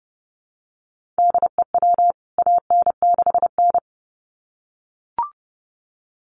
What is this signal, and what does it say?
Morse code, heard as a keyed tone, about 30 wpm, 700 Hz. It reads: DEW AN6N
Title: CWT Exchange 30wpm